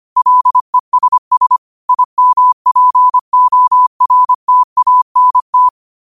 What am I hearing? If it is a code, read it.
Morse code, heard as a keyed tone, about 25 wpm, 1000 Hz. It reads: LESS IMPORTANT